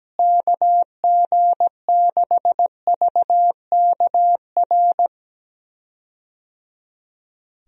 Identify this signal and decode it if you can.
Morse code, heard as a keyed tone, about 17 wpm, 700 Hz. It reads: KG6VKR
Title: USA Call Signs - 6 Characters - Repeated 3x - 17wpm